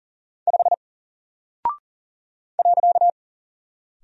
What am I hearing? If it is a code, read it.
Morse code, heard as a keyed tone, about 39 wpm, 700 Hz. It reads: 5 .
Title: Single Character (Letter:Number:Punctuation:Prosign) 40wpm